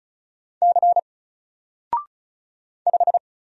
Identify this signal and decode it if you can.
Morse code, heard as a keyed tone, about 36 wpm, 700 Hz. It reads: C 5
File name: Single-Letter-Number-35wpm